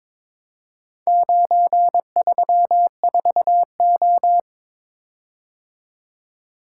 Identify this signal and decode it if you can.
Morse code, heard as a keyed tone, about 22 wpm, 700 Hz. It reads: 934O
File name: practice / 22wpm / Four Letter-Number Combinations - 22wpm